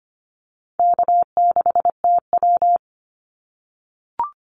Morse code, heard as a keyed tone, about 25 wpm, 700 Hz. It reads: K6TW
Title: NA SOTA Activator Callsigns 25wpm